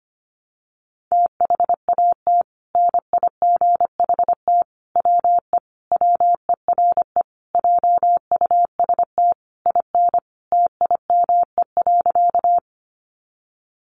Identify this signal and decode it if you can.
Morse code, heard as a keyed tone, about 25 wpm, 700 Hz. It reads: THAT NIGHT WE WERE JUST IN TIME.
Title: Sentences from Top 300 Words 25wpm